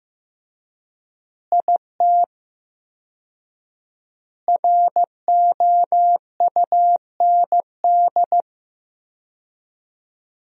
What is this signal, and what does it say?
Morse code, heard as a keyed tone, about 15 wpm, 700 Hz. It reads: IT ROUND
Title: 4x Word-Spacing - Sentences from Top 500 Words - 15wpm